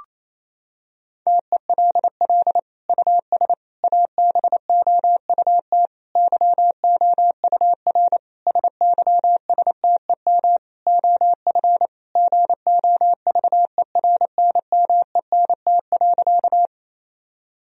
Morse code, heard as a keyed tone, about 28 wpm, 700 Hz. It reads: TELL US ABOUT YOUR SYSTEM OF GOVERNMENT.
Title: Sentences from Top 1000 Words - Review Part 3 - 28wpm